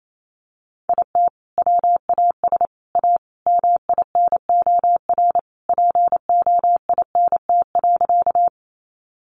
Morse code, heard as a keyed tone, about 28 wpm, 700 Hz. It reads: IT WAS A MINOR POINT.
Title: Sentences from Top 1500 Words 28wpm